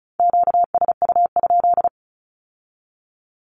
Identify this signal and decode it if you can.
Morse code, heard as a keyed tone, about 36 wpm, 700 Hz. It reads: QSU?
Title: Q-Codes with Spoken Meaning - 35wpm